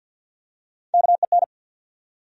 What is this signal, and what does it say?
Morse code, heard as a keyed tone, about 54 wpm, 700 Hz. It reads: KEN
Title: USA Common Names - Max 4 Letters - 50wpm